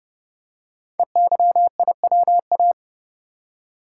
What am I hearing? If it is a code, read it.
Morse code, heard as a keyed tone, about 30 wpm, 700 Hz. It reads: EYIWA
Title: Code Groups - 5 Letters - 30wpm